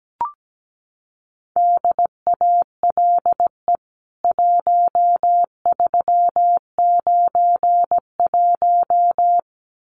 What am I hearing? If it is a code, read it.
Morse code, heard as a keyed tone, about 17 wpm, 700 Hz. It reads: DALE 1391
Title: CWT Exchange 17wpm